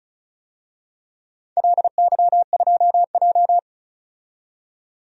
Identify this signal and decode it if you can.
Morse code, heard as a keyed tone, about 36 wpm, 700 Hz. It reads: LY2J
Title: DXCC-100 Call Signs - 4 Characters - 35wpm